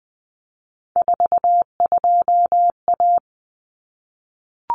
Morse code, heard as a keyed tone, about 20 wpm, 700 Hz. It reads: 42A
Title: Common Cut Numbers - 3-Digit 20wpm